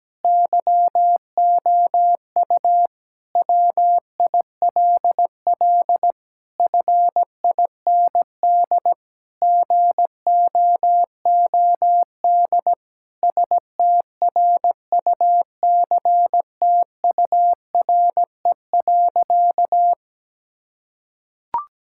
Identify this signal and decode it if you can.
Morse code, heard as a keyed tone, about 17 wpm, 700 Hz. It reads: YOU WILL FIND GOOD STRUCTURE.
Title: Sentences from Top 1000 Words - Review Part 6 - 17wpm